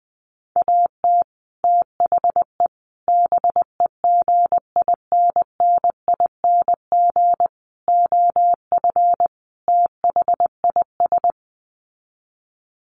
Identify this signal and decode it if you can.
Morse code, heard as a keyed tone, about 20 wpm, 700 Hz. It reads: AT THE BEGINNING OF THIS
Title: Sets of 5 Words Encore 20wpm